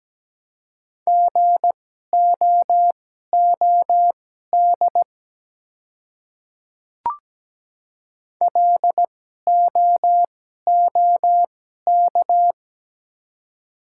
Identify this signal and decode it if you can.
Morse code, heard as a keyed tone, about 17 wpm, spaced as for 12 wpm, 700 Hz. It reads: GOOD LOOK
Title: 2x Character Spacing - Top 200 Words - 17wpm